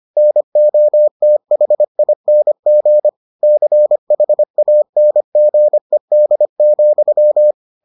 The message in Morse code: NOTHING CHANGED,

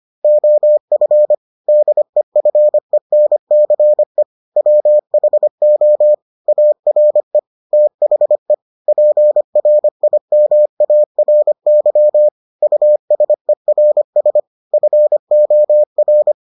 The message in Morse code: OF DEFENCE WHO ARE THE PRIMARY USERS FOR

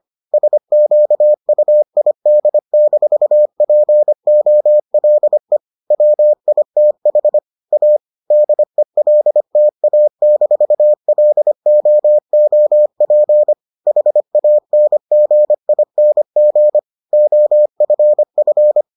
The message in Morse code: SQUID-POLE WITH A DELTA-LOOP HANGING OFF

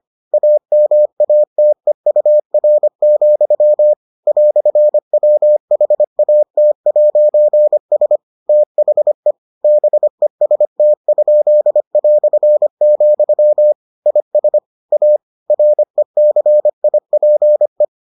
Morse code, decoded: AMATEUR, "WHAT'S THE BEST?", IS A RECIPE

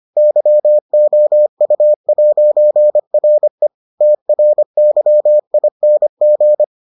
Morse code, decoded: YOU'RE TRYING

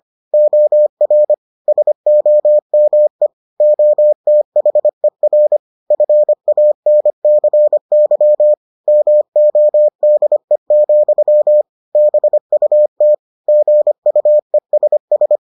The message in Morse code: OR SOME OTHER FANCY MODE, BUT GUESS

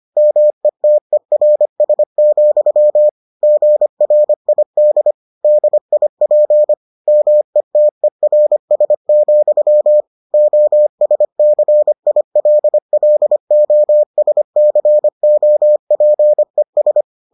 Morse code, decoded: METERS, GRID DIP METERS, OSCILLOSCOPES

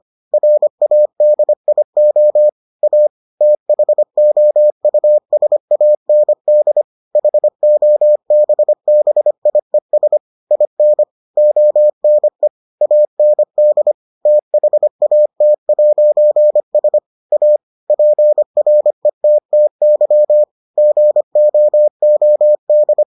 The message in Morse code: RADIO A THOUSAND HOBBIES IN ONE AND THAT'S A PRETTY GOOD

